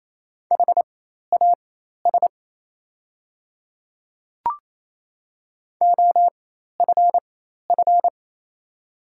Morse code, heard as a keyed tone, about 27 wpm, spaced as for 13 wpm, 700 Hz. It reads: HAS OFF
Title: 4x Character Spacing - Top 200 Words - 28wpm